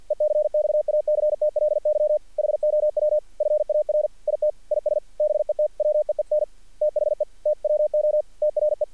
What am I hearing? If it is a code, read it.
Morse code, heard as a keyed tone, about 48 wpm, 610 Hz. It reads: ECXACTLY HOW FAR IT IS BETWEEN THE TWO TRE
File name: archived / CRSnet